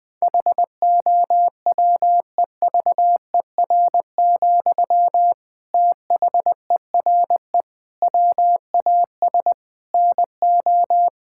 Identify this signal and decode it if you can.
Morse code, heard as a keyed tone, about 20 wpm, 700 Hz. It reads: HOWEVER, THERE WAS NO
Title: Sentences from Top 1600 Words 20wpm